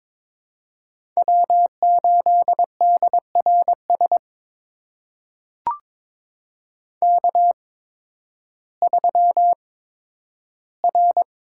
Morse code, heard as a keyed tone, about 22 wpm, spaced as for 12 wpm, 700 Hz. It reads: W8DRS K3R
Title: USA Call Signs - 5 Characters - 8x Character Spacing - 22wpm